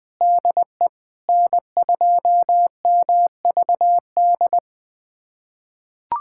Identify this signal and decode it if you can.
Morse code, heard as a keyed tone, about 20 wpm, 700 Hz. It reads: DE N2MVD
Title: USA Callsign DE Callsign - 5 Characters - 20wpm